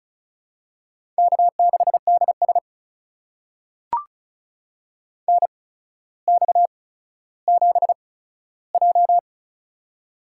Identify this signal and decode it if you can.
Morse code, heard as a keyed tone, about 36 wpm, spaced as for 15 wpm, 700 Hz. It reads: K6DS NX7J
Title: USA Call Signs - 4 Characters - 8x Character Spacing - 35wpm